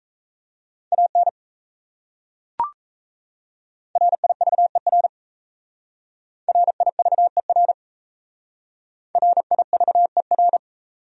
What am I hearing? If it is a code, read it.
Morse code, heard as a keyed tone, about 36 wpm, 700 Hz. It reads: AN RIVER RIVER RIVER